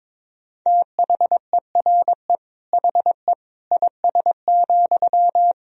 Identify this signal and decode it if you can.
Morse code, heard as a keyed tone, about 22 wpm, 700 Hz. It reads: THERE HE IS,
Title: Sentences from Top 100 Words and Abbreviations 22wpm